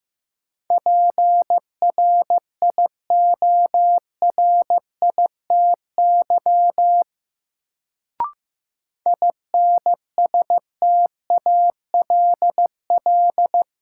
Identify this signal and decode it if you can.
Morse code, heard as a keyed tone, about 15 wpm, 700 Hz. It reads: PRIORITY INSTALL
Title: Sentences from Top 1600 Words - New Word Drill Alt - 15wpm